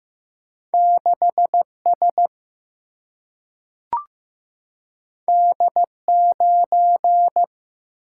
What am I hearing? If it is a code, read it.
Morse code, heard as a keyed tone, about 15 wpm, 700 Hz. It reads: 6 S D 9
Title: Single-Letter Single-Number 15wpm